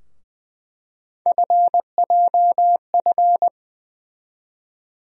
Morse code, heard as a keyed tone, about 20 wpm, 700 Hz. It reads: FJF
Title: USA Call Sign Postfixes - 3 Letters 20wpm